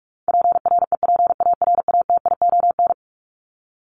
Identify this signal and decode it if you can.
Morse code, heard as a keyed tone, about 44 wpm, 700 Hz. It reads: PREPARATION
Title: Top 1251-1500 Words 45wpm